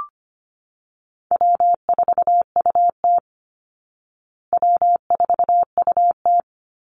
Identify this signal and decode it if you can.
Morse code, heard as a keyed tone, about 25 wpm, 700 Hz. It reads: W4UT W4UT
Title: Easy Call Signs Repeated 3x - 25wpm